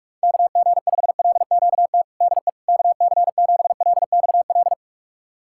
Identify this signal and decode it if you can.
Morse code, heard as a keyed tone, about 46 wpm, 700 Hz. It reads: KG5LQT DE KC7LXL
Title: USA Callsign DE Callsign - 6 Characters - 45wpm